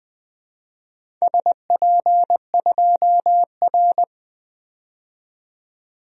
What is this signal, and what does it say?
Morse code, heard as a keyed tone, about 20 wpm, 700 Hz. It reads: SP2R
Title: DXCC-100 Call Signs - 4 Characters - 20wpm